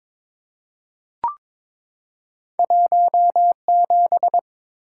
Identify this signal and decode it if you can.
Morse code, heard as a keyed tone, about 22 wpm, 700 Hz. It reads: 17
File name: Two Letter-Number Combinations - 22wpm